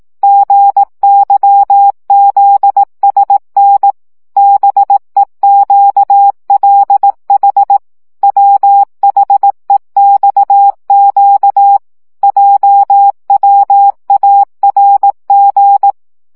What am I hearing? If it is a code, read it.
Morse code, heard as a keyed tone, about 18 wpm, 800 Hz. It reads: GYZSN BEQLH WHEXQ JWARG
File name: morze vizsga